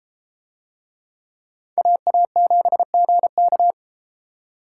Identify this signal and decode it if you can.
Morse code, heard as a keyed tone, about 29 wpm, 700 Hz. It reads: AA7GK